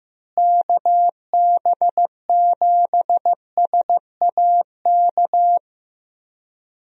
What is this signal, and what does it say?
Morse code, heard as a keyed tone, about 15 wpm, 700 Hz. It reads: KB7SAK
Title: USA Call Signs - 6 Characters - Repeated 2x - 15wpm